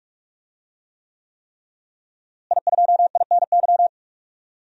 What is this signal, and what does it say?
Morse code, heard as a keyed tone, about 46 wpm, 700 Hz. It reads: I2INY
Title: DXCC-100 Call Signs - 5 Characters - Repeated 3x - 45wpm